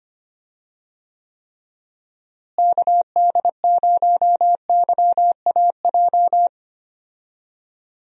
Morse code, heard as a keyed tone, about 25 wpm, 700 Hz. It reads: KD0YAJ
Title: USA Call Signs - 6 Characters - Repeated 3x - 25wpm